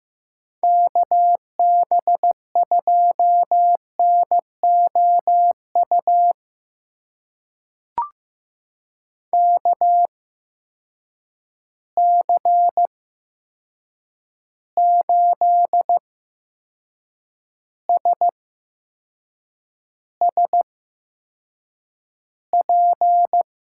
Morse code, heard as a keyed tone, about 15 wpm, spaced as for 6 wpm, 700 Hz. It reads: KB2NOU KC8SSP
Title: USA Call Signs - 6 Characters - 8x Character Spacing - 15wpm